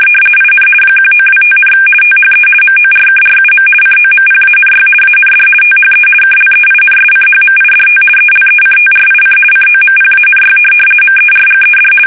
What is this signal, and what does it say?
RTTY, about 50 baud. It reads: IN 1925, THE MORKRUM COMPANY AND THE KLEINSCHMIDT ELECTRIC COMPANY MERGED
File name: fsk850-50B / teletype-corp